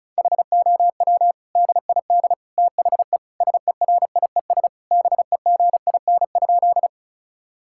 Morse code, heard as a keyed tone, about 36 wpm, 700 Hz. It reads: HOW DID THE SERIES BEGIN?
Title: Sentences from Top 1000 Words - Review Part 5 - 35wpm